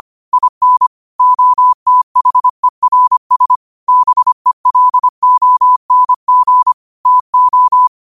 Morse code, heard as a keyed tone, about 25 wpm, 1000 Hz. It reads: IN OTHERS BELONG TO